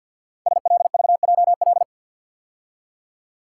Morse code, heard as a keyed tone, about 49 wpm, 700 Hz. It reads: SLVJL
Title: Code Groups - 5 Letters - 50wpm